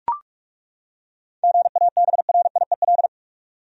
Morse code, heard as a keyed tone, about 43 wpm, 700 Hz. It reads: GABRIEL
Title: USA Common Names - 45wpm